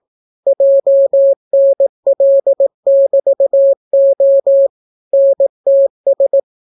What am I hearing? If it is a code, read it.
Morse code, heard as a keyed tone, about 18 wpm, 550 Hz. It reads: JNL<BT>O NTS